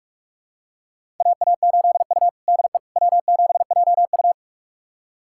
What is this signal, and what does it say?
Morse code, heard as a keyed tone, about 46 wpm, 700 Hz. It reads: AA8U DE W7JU
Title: USA Callsign DE Callsign 45wpm